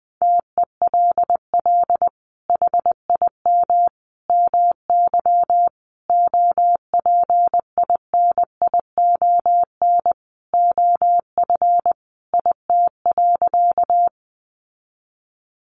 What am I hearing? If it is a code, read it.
Morse code, heard as a keyed tone, about 20 wpm, 700 Hz. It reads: TELL HIM MY OPINION OF IT.
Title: Sentences from Top 1200 Words 20wpm